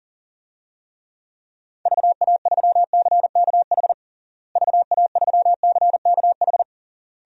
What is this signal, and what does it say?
Morse code, heard as a keyed tone, about 37 wpm, 700 Hz. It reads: VA3CKH VA3CKH
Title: DXCC-100 Call Signs - 6 Characters - Repeated 3x - 40wpm